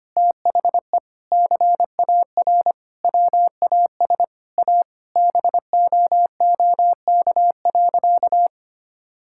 Morse code, heard as a keyed tone, about 25 wpm, 700 Hz. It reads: THE CAR WAS A BOOK.
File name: practice / 25wpm / Sentences with 4-Word Phrases - 25wpm